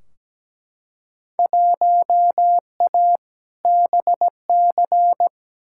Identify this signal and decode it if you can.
Morse code, heard as a keyed tone, about 17 wpm, 700 Hz. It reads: 1A BC
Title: Field Day Exchanges - 17wpm